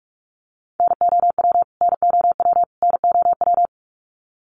Morse code, heard as a keyed tone, about 46 wpm, 700 Hz. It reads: NOW NOW NOW
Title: Top 300 Words Repeated 45wpm